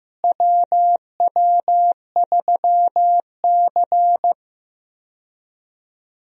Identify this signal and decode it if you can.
Morse code, heard as a keyed tone, about 15 wpm, 700 Hz. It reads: WW3C
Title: USA Call Signs - 4 Characters - Repeated 3x - 15wpm